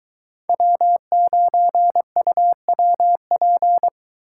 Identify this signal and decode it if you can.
Morse code, heard as a keyed tone, about 24 wpm, 700 Hz. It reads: W9UWP